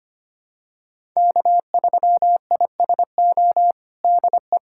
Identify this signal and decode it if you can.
Morse code, heard as a keyed tone, about 25 wpm, 700 Hz. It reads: K3ISO DE
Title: USA Callsign DE Callsign - 5 Characters - 25wpm